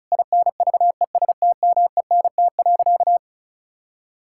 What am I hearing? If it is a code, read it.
Morse code, heard as a keyed tone, about 36 wpm, 700 Hz. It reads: INVESTMENT.
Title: Sentences from Top 1000 Words - Review Part 5 - 35wpm